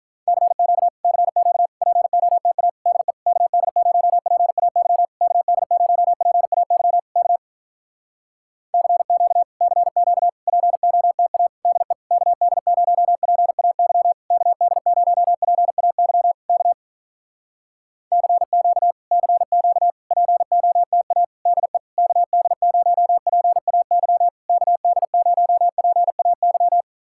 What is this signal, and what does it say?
Morse code, heard as a keyed tone, about 46 wpm, 700 Hz. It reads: CQ CQ POTA DE KD0PAY KD0PAY K CQ CQ POTA DE KD0PAY KD0PAY K CQ CQ POTA DE KD0PAY KD0PAY